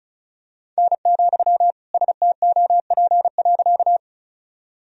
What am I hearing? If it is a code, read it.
Morse code, heard as a keyed tone, about 36 wpm, 700 Hz. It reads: N, STOP.
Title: Sentences from Top 300 Words and Abbreviations 35wpm